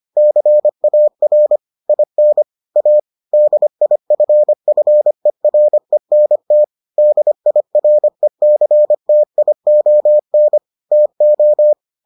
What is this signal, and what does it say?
Morse code, heard as a keyed tone, about 25 wpm, 600 Hz. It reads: CAR IN A DIFFERENT DIRECTION TO